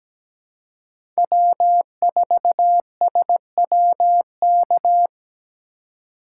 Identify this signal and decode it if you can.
Morse code, heard as a keyed tone, about 17 wpm, 700 Hz. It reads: W4SWK
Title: USA Call Signs - 2x Character Spacing - 17wpm